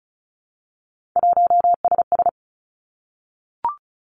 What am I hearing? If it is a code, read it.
Morse code, heard as a keyed tone, about 36 wpm, 700 Hz. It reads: 1SS
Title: Three Challenging Letter-Number Combinations 35wpm